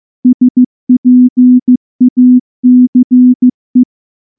Morse code, heard as a keyed tone, about 14 wpm, 250 Hz. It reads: SPACE
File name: morse / morse-training-15WPM